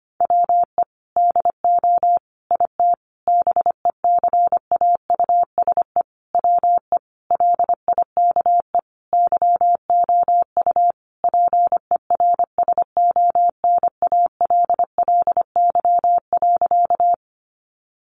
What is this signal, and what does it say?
Morse code, heard as a keyed tone, about 25 wpm, 700 Hz. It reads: WE DO IT BECAUSE WE LIKE YOU PERSONALLY.
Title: Sentences from Top 1400 Words 25wpm